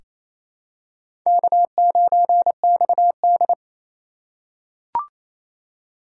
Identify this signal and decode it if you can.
Morse code, heard as a keyed tone, about 28 wpm, 700 Hz. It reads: K9XD
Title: K1USN SST QSO Practice - 28wpm